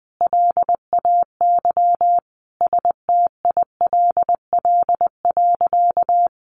Morse code, heard as a keyed tone, about 20 wpm, 700 Hz. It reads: LAY STILL.
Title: A Sound Of Thunder 20wpm